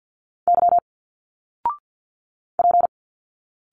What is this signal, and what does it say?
Morse code, heard as a keyed tone, about 51 wpm, 700 Hz. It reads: Y P
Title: Single Letters 50wpm